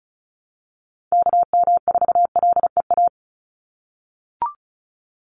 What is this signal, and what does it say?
Morse code, heard as a keyed tone, about 34 wpm, 700 Hz. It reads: KM4LEA
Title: USA Call Signs 35wpm